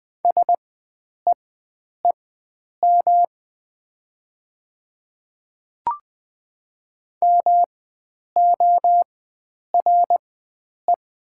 Morse code, heard as a keyed tone, about 20 wpm, spaced as for 9 wpm, 700 Hz. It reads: SEEM MORE
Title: 4x Character Spacing - Top 300 Words - 20wpm